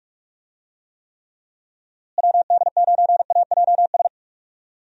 Morse code, heard as a keyed tone, about 43 wpm, 700 Hz. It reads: WD9AJS
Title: USA Call Signs - 6 Characters - Repeated 3x - 45wpm